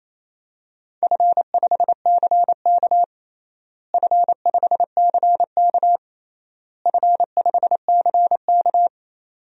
Morse code, heard as a keyed tone, about 27 wpm, 700 Hz. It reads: F5CK F5CK F5CK
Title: DXCC-100 Call Signs - 4 Characters - Repeated 3x - 28wpm